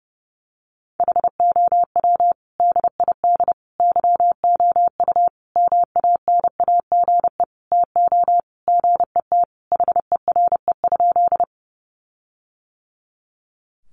Morse code, heard as a keyed tone, about 30 wpm, 700 Hz. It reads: HOW DID YOU MANAGE TO GET HERE?
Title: Sentences from Top 1400 Words 30wpm